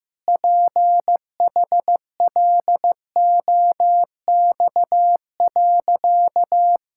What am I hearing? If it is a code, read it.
Morse code, heard as a keyed tone, about 15 wpm, 700 Hz. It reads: PHLOX.
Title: Pangram Challenge v2 - 15wpm